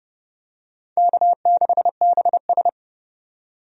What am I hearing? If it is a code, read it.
Morse code, heard as a keyed tone, about 30 wpm, 700 Hz. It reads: K6BS
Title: USA Call Signs - 4 Characters - 30wpm